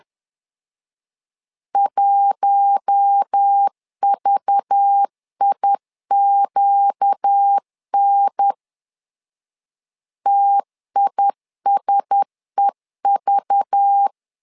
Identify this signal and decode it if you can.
Morse code, heard as a keyed tone, about 10 wpm, 800 Hz. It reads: 1VIQN TISEV
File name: LEZIONI / 5 LBVC / 5 LBV 10wpm 2248